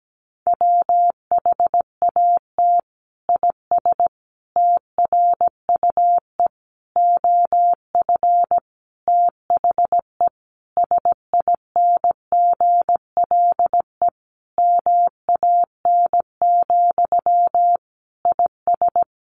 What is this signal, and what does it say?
Morse code, heard as a keyed tone, about 17 wpm, 700 Hz. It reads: WHAT IS TRUE OF THE SINGLE MAN, IS
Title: Sentences from Top 1400 Words 17wpm